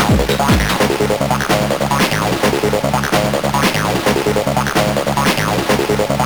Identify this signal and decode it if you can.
There is no code.